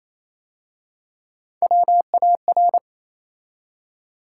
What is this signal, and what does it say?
Morse code, heard as a keyed tone, about 27 wpm, 690 Hz. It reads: WAR
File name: Easy Words - Speed-Racing with Hints - 22wpm